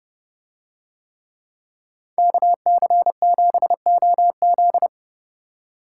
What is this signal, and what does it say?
Morse code, heard as a keyed tone, about 30 wpm, 700 Hz. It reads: KC7OZ